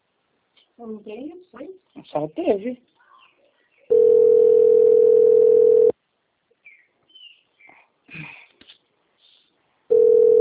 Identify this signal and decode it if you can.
Ringback tone (North America), 440 + 480 Hz, with speech.
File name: out- - - - - .32947